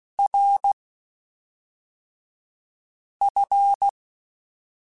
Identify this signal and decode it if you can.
Morse code, heard as a keyed tone, about 16 wpm, spaced as for 3 wpm, 800 Hz. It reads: RF
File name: Technical Subjects / Technical Subjects-mp3s / Sw Pwr Sup @5